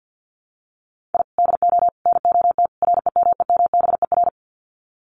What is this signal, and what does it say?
Morse code, heard as a keyed tone, about 50 wpm, 700 Hz. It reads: I DO NOT REMEMBER
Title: Sets of 4 Words 50wpm